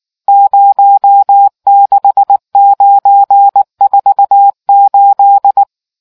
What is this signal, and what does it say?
Morse code, heard as a keyed tone, about 19 wpm, 800 Hz. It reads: 06948